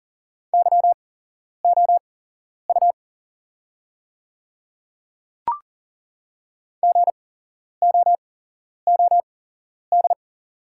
Morse code, heard as a keyed tone, about 40 wpm, spaced as for 11 wpm, 700 Hz. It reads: YOU GOOD
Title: 8x Character Spacing - Top 300 Words - 40wpm